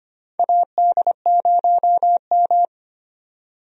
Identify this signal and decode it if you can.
Morse code, heard as a keyed tone, about 25 wpm, 700 Hz. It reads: AD0M
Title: USA Call Signs - 4 Characters - 25wpm